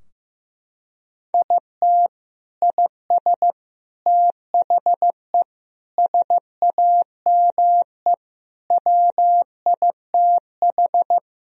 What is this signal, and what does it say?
Morse code, heard as a keyed tone, about 15 wpm, 700 Hz. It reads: IT IS THE SAME WITH
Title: Sentences from Top 1000 Words - Review Part 3 - 15wpm